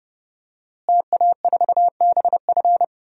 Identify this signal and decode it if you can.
Morse code, heard as a keyed tone, about 30 wpm, 700 Hz. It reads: TA4BF
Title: DXCC-100 Call Signs - 5 Characters - Repeated 2x - 30wpm